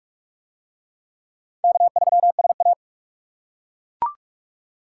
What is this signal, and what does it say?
Morse code, heard as a keyed tone, about 41 wpm, 700 Hz. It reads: K3SA